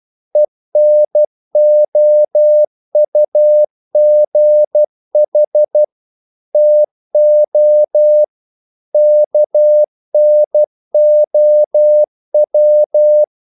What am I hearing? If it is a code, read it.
Morse code, heard as a keyed tone, about 12 wpm, 600 Hz. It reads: ENOUGH TO KNOW
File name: CW / Practice / EnglishQuotations0312